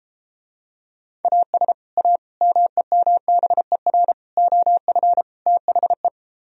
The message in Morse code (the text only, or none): AS A MEMBER OF THE